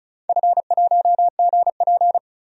Morse code, heard as a keyed tone, about 36 wpm, 690 Hz. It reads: F1GP
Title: DXCC-100 Call Signs - 4 Characters - 35wpm